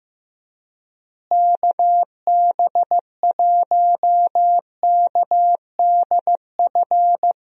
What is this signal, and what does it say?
Morse code, heard as a keyed tone, about 15 wpm, 700 Hz. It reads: KB1KDF
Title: USA Call Signs - 6 Characters - Repeated 3x - 15wpm